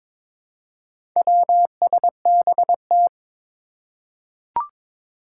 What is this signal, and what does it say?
Morse code, heard as a keyed tone, about 22 wpm, 700 Hz. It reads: WSBT
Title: Four Challenging Letter-Number Combinations 22wpm